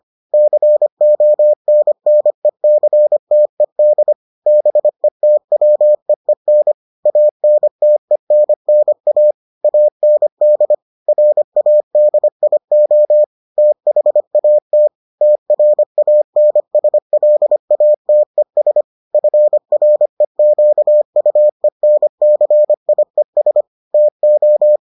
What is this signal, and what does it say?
Morse code, heard as a keyed tone, about 25 wpm, 600 Hz. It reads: CONNECTED BETWEEN ANTENNA AND RADIO THAT TRANSLATES FREQUENCIES TO